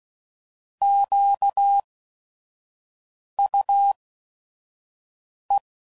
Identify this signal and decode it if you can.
Morse code, heard as a keyed tone, about 16 wpm, spaced as for 5 wpm, 800 Hz. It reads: QUE
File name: Prefixes / By First Letter / Qs